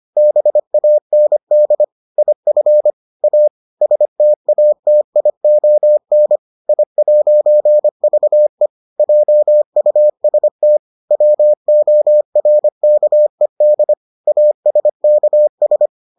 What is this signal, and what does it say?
Morse code, heard as a keyed tone, about 25 wpm, 600 Hz. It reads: BAND IF A STATION I'VE JUST WORKED ASKS